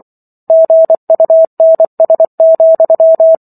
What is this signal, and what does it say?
Morse code, heard as a keyed tone, about 24 wpm, 650 Hz. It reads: GUNS,